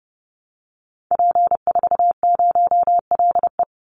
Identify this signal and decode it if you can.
Morse code, heard as a keyed tone, about 29 wpm, 700 Hz. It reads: P40LE
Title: Call Signs Repeated 3x - 30wpm